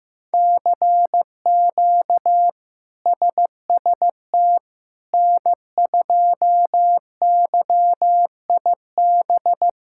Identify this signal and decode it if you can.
Morse code, heard as a keyed tone, about 16 wpm, 700 Hz. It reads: CQ SST N2YIB
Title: K1USN SST QSO Practice - 15wpm